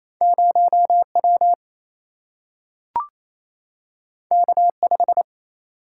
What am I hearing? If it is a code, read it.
Morse code, heard as a keyed tone, about 28 wpm, 700 Hz. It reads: 0 W K 5
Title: Single-Letter Single-Number 28wpm